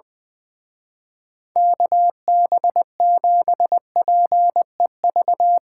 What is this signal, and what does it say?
Morse code, heard as a keyed tone, about 21 wpm, 700 Hz. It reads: KB7PEV
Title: Speed Racing - USA 2x3 Call Signs 20wpm